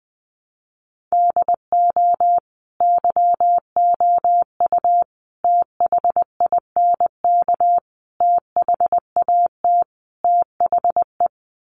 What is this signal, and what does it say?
Morse code, heard as a keyed tone, about 20 wpm, 700 Hz. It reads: DO YOU THINK THAT THE
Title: Sets of 5 Words Encore 20wpm